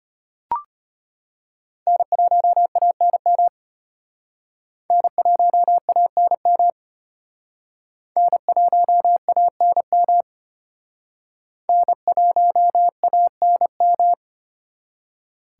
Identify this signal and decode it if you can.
Morse code, heard as a keyed tone, about 34 wpm, 700 Hz. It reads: N1ANM N1ANM N1ANM N1ANM